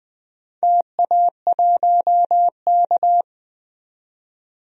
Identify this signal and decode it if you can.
Morse code, heard as a keyed tone, about 20 wpm, 700 Hz. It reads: TA1K
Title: DXCC-100 Call Signs - 4 Characters - 20wpm